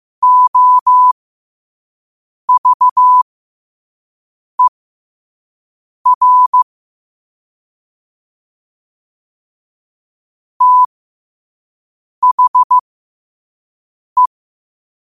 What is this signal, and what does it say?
Morse code, heard as a keyed tone, about 15 wpm, spaced as for 5 wpm, 1000 Hz. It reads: OVER THE
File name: qotdfiles / 5wpm